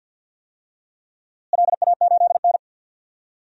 Morse code, heard as a keyed tone, about 53 wpm, 700 Hz. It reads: LA8N
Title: DXCC-100 Call Signs - 4 Characters - Repeated 3x - 50wpm